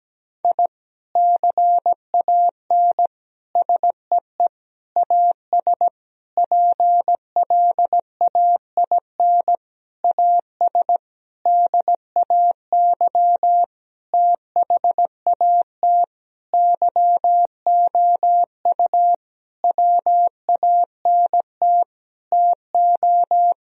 Morse code, heard as a keyed tone, about 17 wpm, 700 Hz. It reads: I CAN SEE AS PLAIN AS DAY THAT YOU WANT TO
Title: Sentences from Top 1000 Words - Review Part 2 - 17wpm